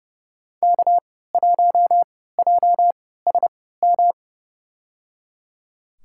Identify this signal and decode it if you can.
Morse code, heard as a keyed tone, about 29 wpm, spaced as for 17 wpm, 700 Hz. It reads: K1JSM
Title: USA Call Signs - 5 Characters - 3x Character Spacing - 30wpm